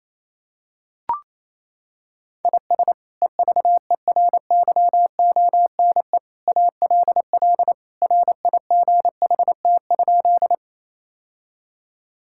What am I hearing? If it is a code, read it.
Morse code, heard as a keyed tone, about 28 wpm, 700 Hz. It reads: IS EVERYONE ALL RIGHT?
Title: Sentences from Top 1000 Words - Review Part 5 - 28wpm